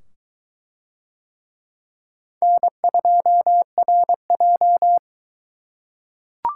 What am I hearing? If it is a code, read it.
Morse code, heard as a keyed tone, about 21 wpm, 700 Hz. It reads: N2RJ